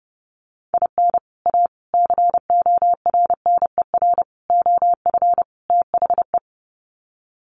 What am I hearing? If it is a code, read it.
Morse code, heard as a keyed tone, about 30 wpm, 700 Hz. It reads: IN A CORNER OF THE
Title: Sets of 5 Words Encore2 30wpm